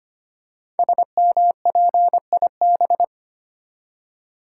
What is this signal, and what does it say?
Morse code, heard as a keyed tone, about 25 wpm, 700 Hz. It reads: SMPIB
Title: Code Groups - 5 Letters - 25wpm